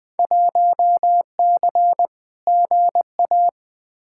Morse code, heard as a keyed tone, about 20 wpm, 700 Hz. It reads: 1C GA
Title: Field Day Exchanges - 20wpm